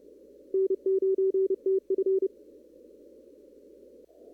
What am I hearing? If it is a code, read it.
Morse code, heard as a keyed tone, about 30 wpm, 380 Hz. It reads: N9TF
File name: CW / kt5j wpx cw-op-k5pi